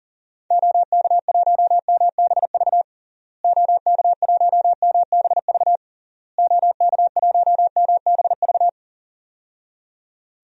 Morse code, heard as a keyed tone, about 40 wpm, 700 Hz. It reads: OK1MBV OK1MBV OK1MBV
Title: DXCC-100 Call Signs - 6 Characters - Repeated 3x - 40wpm